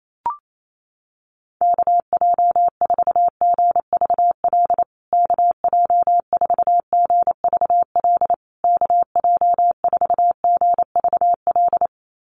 Morse code, heard as a keyed tone, about 28 wpm, 700 Hz. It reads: KJ4GVL KJ4GVL KJ4GVL
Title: USA Call Signs Repeated 3x 28wpm